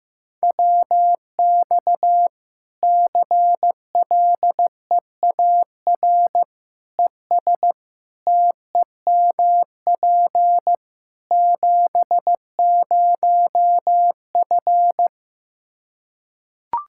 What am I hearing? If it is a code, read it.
Morse code, heard as a keyed tone, about 15 wpm, 700 Hz. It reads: WX CLEAR ES TEMP 70F
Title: Practice QSO - Exchange 2 - Info - 15wpm